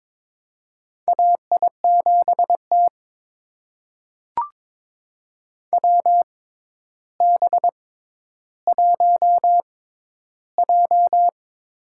Morse code, heard as a keyed tone, about 22 wpm, spaced as for 11 wpm, 700 Hz. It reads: AI7T WB1J
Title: USA Call Signs - 4 Characters - 6x Character Spacing - 22wpm